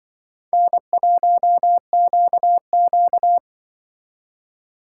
Morse code, heard as a keyed tone, about 25 wpm, 700 Hz. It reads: N1QQ
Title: Speed Racing - USA 1x2 Call Signs 20wpm